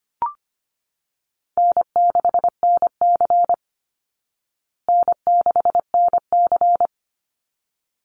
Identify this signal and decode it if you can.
Morse code, heard as a keyed tone, about 25 wpm, 700 Hz. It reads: N6NC N6NC
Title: Easy Call Signs Repeated 2x - 25wpm